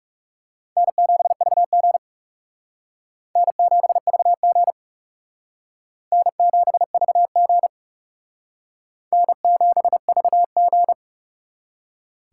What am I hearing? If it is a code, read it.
Morse code, heard as a keyed tone, about 37 wpm, 700 Hz. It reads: N7VG N7VG N7VG N7VG
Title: Speed Racing - USA 1x2 Call Signs 30wpm